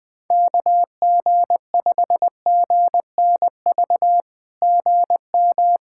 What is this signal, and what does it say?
Morse code, heard as a keyed tone, about 19 wpm, 700 Hz. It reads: KG5GNV GM
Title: SOTA QSO Practice 20wpm